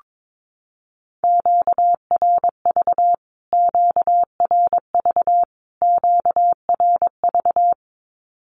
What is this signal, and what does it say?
Morse code, heard as a keyed tone, about 22 wpm, 700 Hz. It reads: QRV QRV QRV
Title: Q-Codes Repeated 3x 22wpm